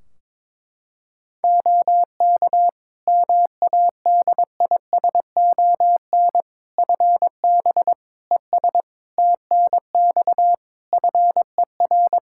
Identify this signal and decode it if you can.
Morse code, heard as a keyed tone, about 22 wpm, 700 Hz. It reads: OK MADISON FB ES TNX FER
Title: Practice QSO - Exchange 2 - Info - 22wpm